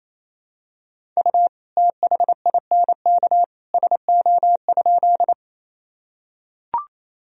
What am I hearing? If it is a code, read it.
Morse code, heard as a keyed tone, about 27 wpm, 700 Hz. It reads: U THINK SO?
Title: Sentences from Top 200 Words and Abbreviations 28wpm